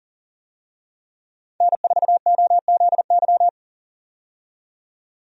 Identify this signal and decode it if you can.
Morse code, heard as a keyed tone, about 39 wpm, 700 Hz. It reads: N4OZY
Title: USA Call Signs - 5 Characters - 40wpm